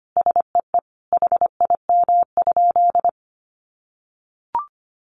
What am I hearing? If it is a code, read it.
Morse code, heard as a keyed tone, about 25 wpm, 700 Hz. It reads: SEE HIM?
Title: Sentences from Top 100 Words 25wpm